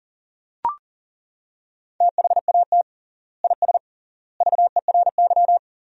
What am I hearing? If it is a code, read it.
Morse code, heard as a keyed tone, about 40 wpm, 700 Hz. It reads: THAT IS VERY
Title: 3x Word-Spacing - Sentences from Top 400 Words - 40wpm